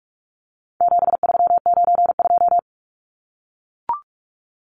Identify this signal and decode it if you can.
Morse code, heard as a keyed tone, about 44 wpm, 700 Hz. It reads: 7392
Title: Numbers - 4-Digit 45wpm